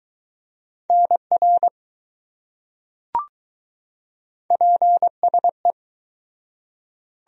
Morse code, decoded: NR PSE

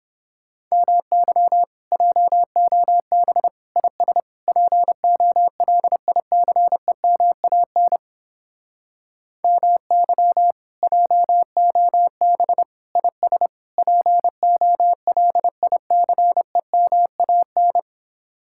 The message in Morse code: MY JOB IS POLICEMAN MY JOB IS POLICEMAN

